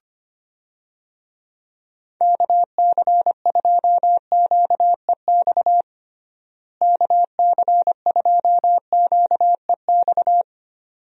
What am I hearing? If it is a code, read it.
Morse code, heard as a keyed tone, about 25 wpm, 700 Hz. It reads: KC2QEX KC2QEX
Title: USA Call Signs - 6 Characters - Repeated 2x - 25wpm